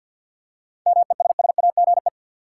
Morse code, heard as a keyed tone, about 53 wpm, 700 Hz. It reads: MESSAGE
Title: Seven Letter Words - 50wpm